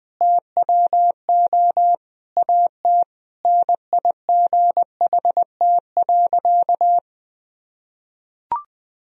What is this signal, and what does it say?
Morse code, heard as a keyed tone, about 21 wpm, 700 Hz. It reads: TWO AT NIGHT.
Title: Sentences from Top 1000 Words - Review Part 1 - 20wpm